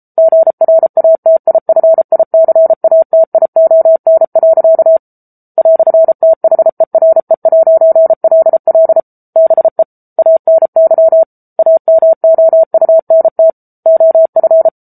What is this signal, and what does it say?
Morse code, heard as a keyed tone, about 34 wpm, 650 Hz. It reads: GRATIFICATION. "THERE'LL BE ANY AMOUNT OF
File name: Book Two / 07-Chapter 7 - 33 WPM